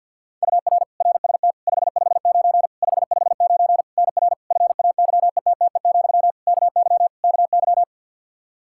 Morse code, heard as a keyed tone, about 52 wpm, 700 Hz. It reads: UR RST 559 559 NR FAYETTE, KY KY
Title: Practice QSO - Exchange 1 - Beginning - 50wpm